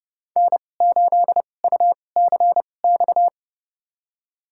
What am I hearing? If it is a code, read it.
Morse code, heard as a keyed tone, about 30 wpm, spaced as for 22 wpm, 700 Hz. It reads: N8UCX
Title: USA Call Signs - 5 Characters - 2x Character Spacing - 30wpm